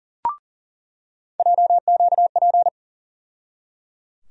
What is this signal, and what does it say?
Morse code, heard as a keyed tone, about 37 wpm, 700 Hz. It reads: JQP
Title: USA Call Sign Postfixes - 3 Letters 40wpm